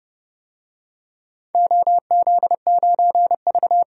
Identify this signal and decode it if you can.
Morse code, heard as a keyed tone, about 29 wpm, 700 Hz. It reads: OZ9V
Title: DXCC-100 Call Signs - 4 Characters - 30wpm